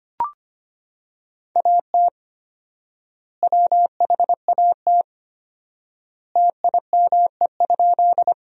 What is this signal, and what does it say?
Morse code, heard as a keyed tone, about 25 wpm, 700 Hz. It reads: AT WHAT TIME?
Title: 4x Word-Spacing - Sentences from Top 100 Words - 25wpm